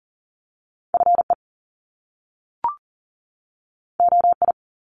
Morse code, heard as a keyed tone, about 40 wpm, 700 Hz. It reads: FE OI
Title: Top Two Letter Combinations in Words 40wpm